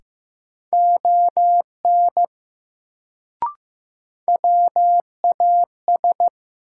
Morse code, heard as a keyed tone, about 15 wpm, 700 Hz. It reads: ON WAS
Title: Sentences from Top 100 Words - New Word Drill - 15wpm